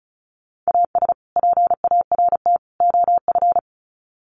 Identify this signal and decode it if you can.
Morse code, heard as a keyed tone, about 34 wpm, 700 Hz. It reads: AS PART OF
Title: Sets of 3 Words 35wpm